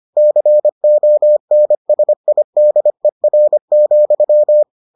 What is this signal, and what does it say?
Morse code, heard as a keyed tone, about 25 wpm, 600 Hz. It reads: CONSIDER,